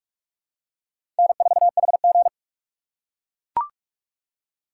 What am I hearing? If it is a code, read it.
Morse code, heard as a keyed tone, about 42 wpm, 700 Hz. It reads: N4HG
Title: USA Call Signs - 3x Character Spacing - 45wpm